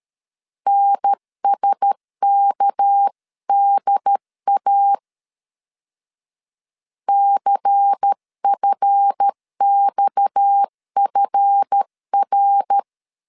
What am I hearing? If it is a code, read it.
Morse code, heard as a keyed tone, about 13 wpm, 800 Hz. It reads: NSKDA CFXFR
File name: LEZIONI / 7 GF9Z / 7 GF 12wpm 2122